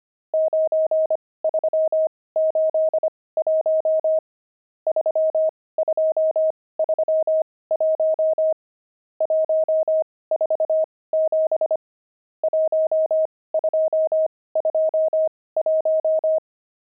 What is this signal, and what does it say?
Morse code, heard as a keyed tone, about 25 wpm, spaced as for 18 wpm, 630 Hz. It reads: 9381 3231 147 1221
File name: pangram-fs-18-cs-25-nums14